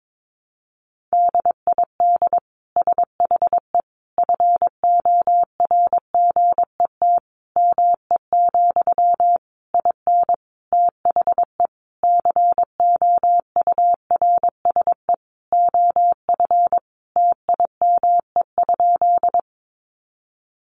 Morse code, heard as a keyed tone, about 22 wpm, 700 Hz. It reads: DID SHE FORGET ME, IN THE COURSE OF TIME?
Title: Sentences from Top 1200 Words 22wpm